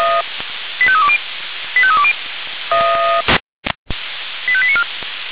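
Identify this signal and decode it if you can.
Five-tone selective call (ZVEI), heard as quick runs of five tones, tones of about 70 ms. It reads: 85320 85320 85894
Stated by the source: ZVEI five-tone call received by SDR.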